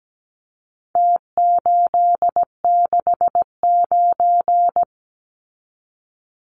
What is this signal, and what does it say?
Morse code, heard as a keyed tone, about 17 wpm, 700 Hz. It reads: T869
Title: Numbers - Contest Style - 4-Digit 17wpm